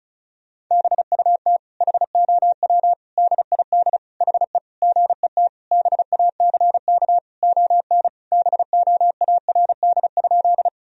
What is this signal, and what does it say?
Morse code, heard as a keyed tone, about 36 wpm, 700 Hz. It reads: BUT HOW DID HE GET BACK ON BOARD?
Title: Sentences from Top 1000 Words - Review Part 4 - 35wpm